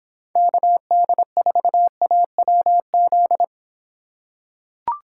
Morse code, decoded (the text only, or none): KD4AWZ